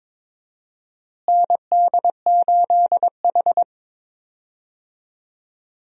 Morse code, heard as a keyed tone, about 22 wpm, 700 Hz. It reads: ND8H
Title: USA Call Signs - 4 Characters - Repeated 3x - 22wpm